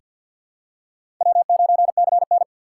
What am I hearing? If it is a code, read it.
Morse code, heard as a keyed tone, about 50 wpm, 700 Hz. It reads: W9CN
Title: USA Call Signs - 6x Character Spacing - 50wpm